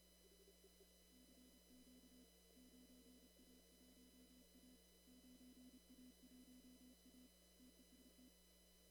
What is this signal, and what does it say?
Morse code, heard as a keyed tone, about 29 wpm, 250 Hz. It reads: CQ 9A1A 9A1A TEST
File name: CW / kt5j wpx cw-op-k5pi